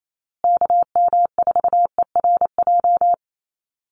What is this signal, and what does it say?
Morse code, heard as a keyed tone, about 28 wpm, 700 Hz. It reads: KM4ERJ
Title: USA Call Signs Repeated 2x 28wpm